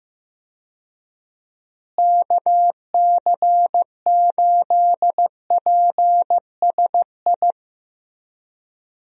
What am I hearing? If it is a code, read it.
Morse code, heard as a keyed tone, about 15 wpm, 700 Hz. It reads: KC8PSI
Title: USA Call Signs - 6 Characters - Repeated 3x - 15wpm